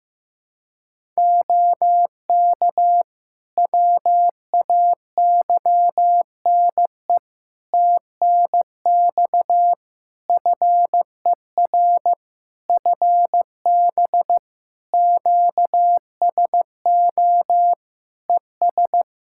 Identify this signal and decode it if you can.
Morse code, heard as a keyed tone, about 15 wpm, 700 Hz. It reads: OK WAYNE TNX FER FB QSO ES
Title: Practice QSO - Exchange 4 - Ending - 15wpm